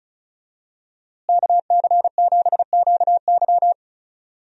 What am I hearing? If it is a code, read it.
Morse code, heard as a keyed tone, about 34 wpm, 700 Hz. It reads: KC7QY
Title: USA Call Signs - 5 Characters - Repeated 2x - 35wpm